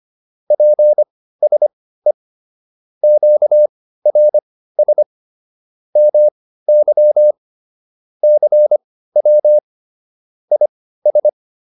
Morse code, heard as a keyed tone, about 25 wpm, spaced as for 15 wpm, 600 Hz. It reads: PSE QRS MY CW IS